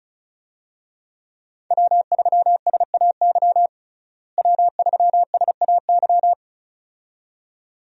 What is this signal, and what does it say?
Morse code, heard as a keyed tone, about 34 wpm, 700 Hz. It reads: W3SAY W3SAY
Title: USA Call Signs - 5 Characters - Repeated 2x - 35wpm